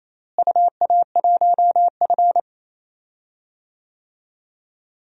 Morse code, heard as a keyed tone, about 27 wpm, 700 Hz. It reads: UA1F
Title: DXCC-100 Call Signs - 4 Characters - Repeated 3x - 28wpm